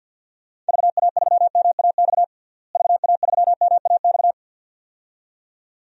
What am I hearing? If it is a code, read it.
Morse code, heard as a keyed tone, about 51 wpm, 700 Hz. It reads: VA3MAX VA3MAX
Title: DXCC-100 Call Signs - 6 Characters - Repeated 2x - 50wpm